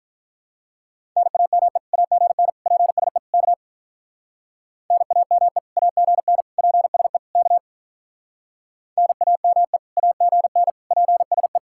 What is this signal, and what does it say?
Morse code, heard as a keyed tone, about 44 wpm, 700 Hz. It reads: NAME AGN PSE K NAME AGN PSE K NAME AGN PSE